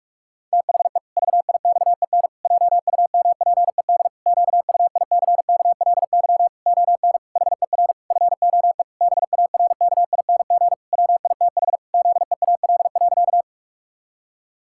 Morse code, heard as a keyed tone, about 46 wpm, 700 Hz. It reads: THE VIXEN JUMPED QUICKLY ON HER FOE BARKING WITH ZEAL.
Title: Pangram Challenge v2 - 45wpm